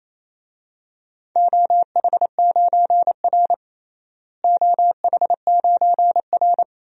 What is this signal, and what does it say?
Morse code, heard as a keyed tone, about 28 wpm, 700 Hz. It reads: OH9R OH9R
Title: DXCC-100 Call Signs - 4 Characters - Repeated 3x - 28wpm